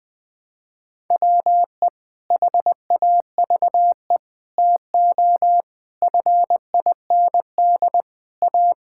Morse code, heard as a keyed tone, about 20 wpm, 700 Hz. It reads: WE HAVE TO FIND A
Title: Sentences with 7-Word Phrases - 20wpm